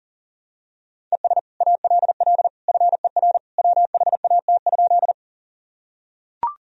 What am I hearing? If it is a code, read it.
Morse code, heard as a keyed tone, about 37 wpm, 700 Hz. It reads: ES ALL FER WHAT?
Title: Sentences from Top 100 Words and Abbreviations 40wpm